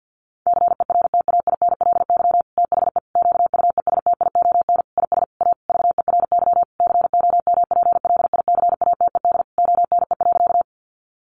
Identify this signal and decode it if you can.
Morse code, heard as a keyed tone, about 50 wpm, 700 Hz. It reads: CERTAINLY THE QUESTION IS A VERY COMPLICATED ONE.
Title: Sentences from Top 1200 Words 50wpm